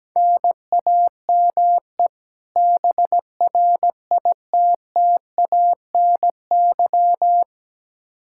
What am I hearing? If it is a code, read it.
Morse code, heard as a keyed tone, about 17 wpm, 700 Hz. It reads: NAME BRITTANY
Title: K1USN SST QSO Practice 17wpm